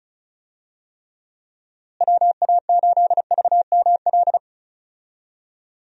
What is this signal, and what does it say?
Morse code, heard as a keyed tone, about 36 wpm, 700 Hz. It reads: WA8VML
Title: USA Call Signs - 6 Characters - Repeated 3x - 35wpm